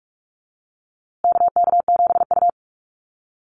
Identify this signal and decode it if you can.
Morse code, heard as a keyed tone, about 44 wpm, 700 Hz. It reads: KK7U
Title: USA Call Signs Repeated 2x 45wpm